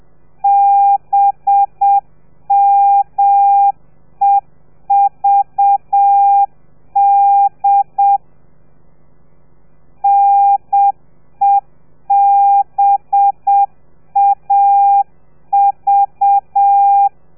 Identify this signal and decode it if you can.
Morse code, heard as a keyed tone, about 7 wpm, 800 Hz. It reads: BMEVD NEBAV
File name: telegrafia / i2tfj / Lez4 30